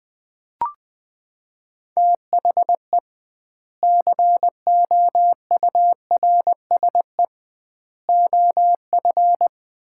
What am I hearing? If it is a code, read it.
Morse code, heard as a keyed tone, about 21 wpm, 700 Hz. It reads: THE COURSE OF
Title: 2x Word-Spacing - Sets of 4 Words - 20wpm